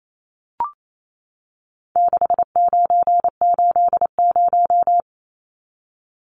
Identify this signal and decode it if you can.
Morse code, heard as a keyed tone, about 28 wpm, 700 Hz. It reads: 6980
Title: Numbers - 4-Digit 28wpm